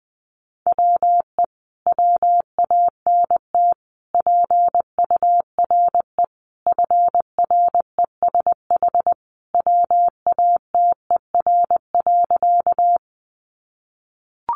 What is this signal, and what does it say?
Morse code, heard as a keyed tone, about 19 wpm, 700 Hz. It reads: WE WANT PURE FRESH WATER.
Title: Sentences from Top 1500 Words 20wpm